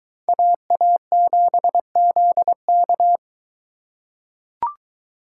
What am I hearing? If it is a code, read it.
Morse code, heard as a keyed tone, about 21 wpm, 700 Hz. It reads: AA7ZK